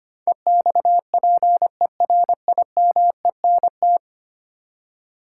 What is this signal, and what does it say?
Morse code, heard as a keyed tone, about 25 wpm, 700 Hz. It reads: EXPERIMENT
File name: 10 Letter Words - First 3 Letters Given - 25wpm